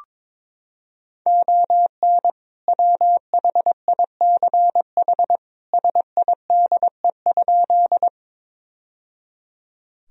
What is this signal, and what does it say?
Morse code, heard as a keyed tone, about 22 wpm, 700 Hz. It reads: ON WHICH SIDE?
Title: Sentences from Top 1000 Words - Review Part 1 - 22wpm